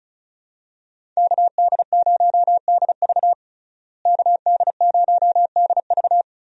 Morse code, heard as a keyed tone, about 36 wpm, 700 Hz. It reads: KD0DV KD0DV
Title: USA Call Signs - 5 Characters - Repeated 3x - 35wpm